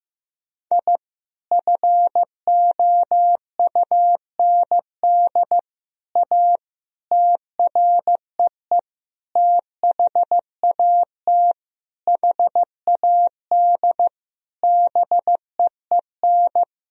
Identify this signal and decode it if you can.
Morse code, heard as a keyed tone, about 15 wpm, 700 Hz. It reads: I FOUND A TREE THAT HAD BEEN
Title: Sentences from Top 1000 Words - Review Part 1 - 15wpm